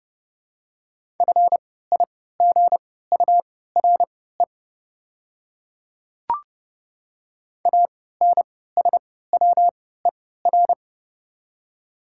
Morse code, heard as a keyed tone, about 30 wpm, spaced as for 17 wpm, 700 Hz. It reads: FIGURE ANSWER
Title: 3x Character Spacing - Top 500 Words - 30wpm